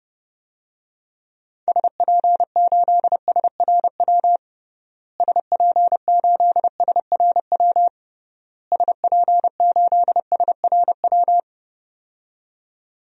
Morse code, heard as a keyed tone, about 32 wpm, 700 Hz. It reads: SP8SRW SP8SRW SP8SRW
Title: DXCC-100 Call Signs - 6 Characters - Repeated 3x - 30wpm